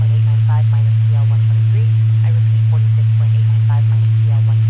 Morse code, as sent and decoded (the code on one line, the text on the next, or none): none